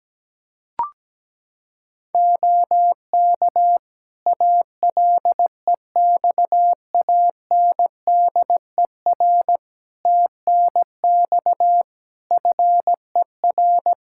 Morse code, heard as a keyed tone, about 17 wpm, 700 Hz. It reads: OK ALEXANDER TNX FER
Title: K1USN SST QSO Practice 17wpm